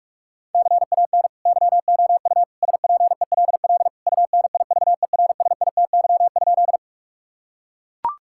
Morse code, heard as a keyed tone, about 42 wpm, 690 Hz. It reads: CAN YOU SPELL UNIVERSITY?
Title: Sentences from Top 1000 Words - Review Part 5 - 45wpm